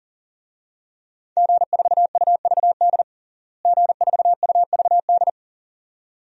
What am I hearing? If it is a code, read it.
Morse code, heard as a keyed tone, about 37 wpm, 700 Hz. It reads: G4UVD G4UVD
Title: DXCC-100 Call Signs - 5 Characters - Repeated 2x - 40wpm